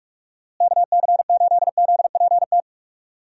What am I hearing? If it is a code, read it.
Morse code, heard as a keyed tone, about 46 wpm, 700 Hz. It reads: KC8ZPT
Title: USA Call Signs - 6 Characters - Repeated 3x - 45wpm